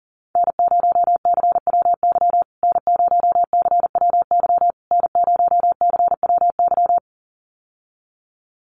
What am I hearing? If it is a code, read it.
Morse code, heard as a keyed tone, about 40 wpm, 700 Hz. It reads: N0CWY N0CWY N0CWY
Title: USA Call Signs Repeated 3x 40wpm